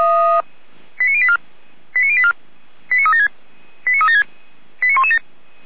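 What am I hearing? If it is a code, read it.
Five-tone selective call (ZVEI), heard as quick runs of five tones, tones of about 70 ms. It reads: 89084 89084 89376 89376 89117
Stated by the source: ZVEI five-tone selective call received by SDR.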